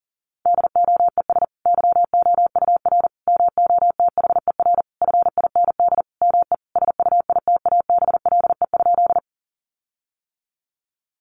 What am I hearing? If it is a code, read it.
Morse code, heard as a keyed tone, about 40 wpm, 700 Hz. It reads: DOES YOUR MOTHER FIND ME SUITABLE?
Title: Sentences from Top 1400 Words 40wpm